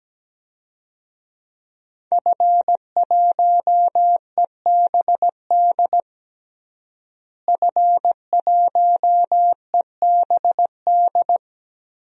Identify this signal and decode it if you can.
Morse code, heard as a keyed tone, about 17 wpm, 700 Hz. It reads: F1EBD F1EBD
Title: DXCC-100 Call Signs - 5 Characters - Repeated 3x - 17wpm